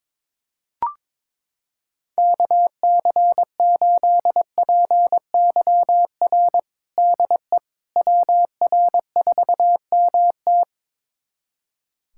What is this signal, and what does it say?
Morse code, heard as a keyed tone, about 21 wpm, 700 Hz. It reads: KC8PYR DE WR4MT
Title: USA Callsign DE Callsign 22wpm